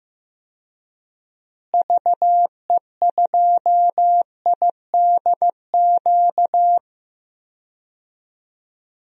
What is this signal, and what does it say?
Morse code, heard as a keyed tone, about 15 wpm, 700 Hz. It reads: VE2IDQ
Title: DXCC-100 Call Signs - 6 Characters - Repeated 2x - 15wpm